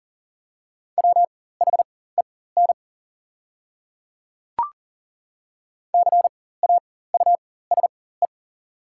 Morse code, heard as a keyed tone, about 40 wpm, spaced as for 19 wpm, 700 Hz. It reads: WHEN CAUSE
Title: 4x Character Spacing - Top 200 Words - 40wpm